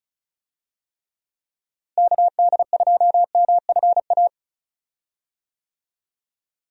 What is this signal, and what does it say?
Morse code, heard as a keyed tone, about 36 wpm, 700 Hz. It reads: KD2MFA
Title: USA Call Signs - 6 Characters - Repeated 3x - 35wpm